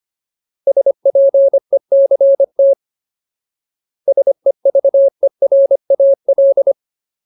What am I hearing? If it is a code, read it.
Morse code, heard as a keyed tone, about 25 wpm, 550 Hz. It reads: SPECT SEVERAL
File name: Ultimate Word List / 25WPM Ultimate